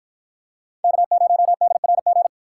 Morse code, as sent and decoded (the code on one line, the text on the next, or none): -.- ----- -.. .-. --.
K0DRG